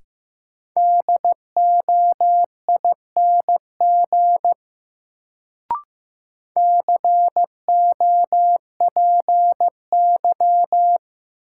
Morse code, heard as a keyed tone, about 16 wpm, 700 Hz. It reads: DOING COPY
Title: Sentences from Top 1400 Words - New Word Drill - 15wpm